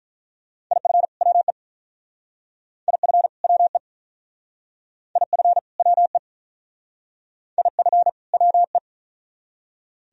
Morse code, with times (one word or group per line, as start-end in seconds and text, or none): 0.7-1.1 IF
1.2-1.5 WE
2.9-3.3 IF
3.4-3.8 WE
5.2-5.6 IF
5.8-6.2 WE
7.6-8.1 IF
8.3-8.8 WE